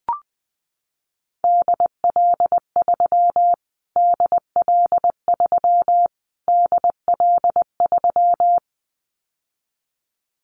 Morse code, heard as a keyed tone, about 20 wpm, 700 Hz. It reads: DL3 DL3 DL3
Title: Callsign Prefixes for Top 25 DXCC - Repeated 3x - 20wpm